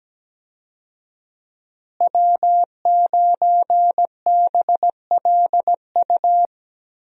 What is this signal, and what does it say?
Morse code, heard as a keyed tone, about 17 wpm, 700 Hz. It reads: W9BLU
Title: USA Call Signs - 5 Characters - Repeated 2x - 17wpm